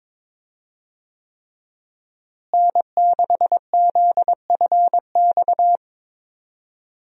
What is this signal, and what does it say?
Morse code, heard as a keyed tone, about 22 wpm, 700 Hz. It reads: N6ZFX
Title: USA Call Signs - 5 Characters - Repeated 3x - 22wpm